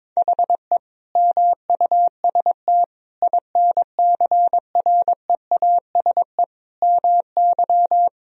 Morse code, heard as a keyed tone, about 22 wpm, 700 Hz. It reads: HE MUST INCREASE MY
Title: Sentences from Top 1000 Words - Review Part 4 - 22wpm